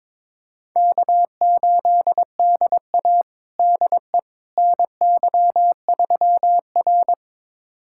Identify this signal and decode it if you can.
Morse code, heard as a keyed tone, about 22 wpm, 700 Hz. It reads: K8DA DE NY3R
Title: USA Callsign DE Callsign - 4 Characters - 22wpm